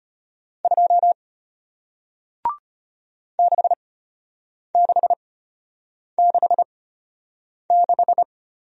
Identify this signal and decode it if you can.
Morse code, heard as a keyed tone, about 34 wpm, spaced as for 8 wpm, 700 Hz. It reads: 2 6666